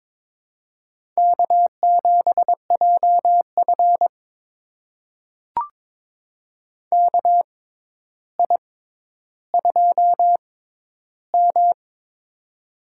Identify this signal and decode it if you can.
Morse code, heard as a keyed tone, about 21 wpm, spaced as for 11 wpm, 700 Hz. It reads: K7JF KI2M
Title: USA Call Signs - 4 Characters - 6x Character Spacing - 22wpm